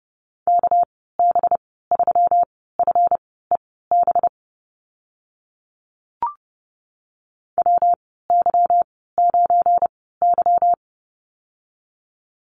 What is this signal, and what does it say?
Morse code, heard as a keyed tone, about 30 wpm, spaced as for 17 wpm, 700 Hz. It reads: KB3FEB WY9Y
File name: Lesson 37g - Letter x - Call Signs 30wpm